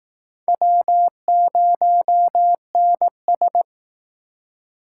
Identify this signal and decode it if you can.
Morse code, heard as a keyed tone, about 19 wpm, 700 Hz. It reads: W0NS